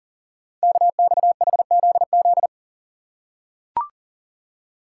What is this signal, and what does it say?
Morse code, heard as a keyed tone, about 38 wpm, 690 Hz. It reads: KXHZZ
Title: Code Groups - 5 Letters - 40wpm